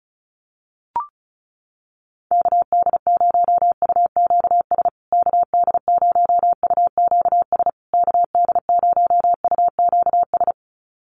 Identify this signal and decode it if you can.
Morse code, heard as a keyed tone, about 34 wpm, 700 Hz. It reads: KD0UQS KD0UQS KD0UQS
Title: USA Call Signs Repeated 3x 35wpm